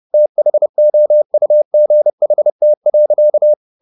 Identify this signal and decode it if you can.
Morse code, heard as a keyed tone, about 30 wpm, 600 Hz. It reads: THOUGHT.
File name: the old man and the sea / the old man and the sea 1 30wpm